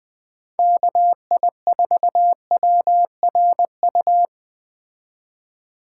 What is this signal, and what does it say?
Morse code, heard as a keyed tone, about 20 wpm, 700 Hz. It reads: KI4WRU